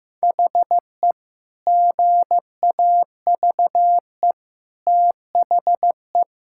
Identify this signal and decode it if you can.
Morse code, heard as a keyed tone, about 15 wpm, 700 Hz. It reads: HE GAVE THE
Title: Sentences from Top 1600 Words 15wpm